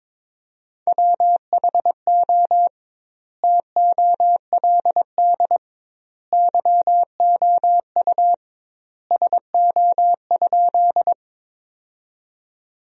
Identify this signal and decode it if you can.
Morse code, heard as a keyed tone, about 21 wpm, 700 Hz. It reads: WHO TOLD YOU SO?
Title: 2x Word-Spacing - Sentences from Top 400 Words - 22wpm